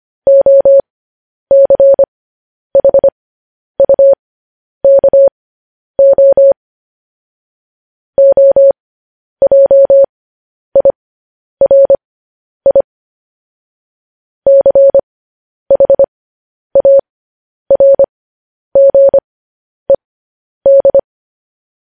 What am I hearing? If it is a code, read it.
Morse code, heard as a keyed tone, about 25 wpm, spaced as for 10 wpm, 550 Hz. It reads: OCHUKO OJIRI CHARGED
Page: News Headlines in Morse Code at 10 WPM